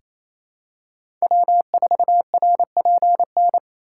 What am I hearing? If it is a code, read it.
Morse code, heard as a keyed tone, about 28 wpm, 700 Hz. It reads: W4RPN
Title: USA Call Signs - 5 Characters - 4x Character Spacing - 28wpm